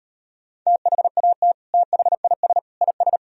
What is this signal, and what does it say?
Morse code, heard as a keyed tone, about 38 wpm, 700 Hz. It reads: THAT THIS IS